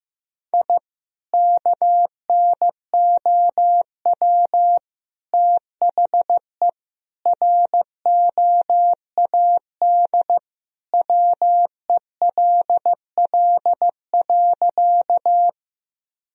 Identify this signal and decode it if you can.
Morse code, heard as a keyed tone, about 15 wpm, 700 Hz. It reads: I KNOW THE ROAD WELL.
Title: Sentences from Top 1000 Words - Review Part 2 - 15wpm